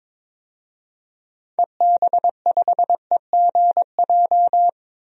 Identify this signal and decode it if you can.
Morse code, heard as a keyed tone, about 22 wpm, 700 Hz. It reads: EB5EGJ
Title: DXCC-100 Call Signs - 6 Characters - Repeated 2x - 22wpm